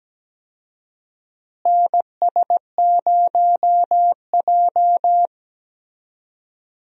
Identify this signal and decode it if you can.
Morse code, heard as a keyed tone, about 17 wpm, 700 Hz. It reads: NS0J
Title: USA Call Signs - 4 Characters - Repeated 2x - 17wpm